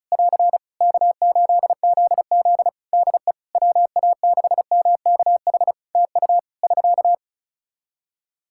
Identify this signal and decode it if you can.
Morse code, heard as a keyed tone, about 34 wpm, 700 Hz. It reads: <AR> K8ZZ DE WA6MKH TU <SK>
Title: K1USN SST QSO Practice 35wpm